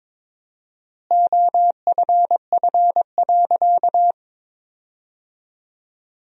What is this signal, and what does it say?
Morse code, heard as a keyed tone, about 21 wpm, 700 Hz. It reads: OFF.
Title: 4x Word-Spacing - Sentences from Top 500 Words - 22wpm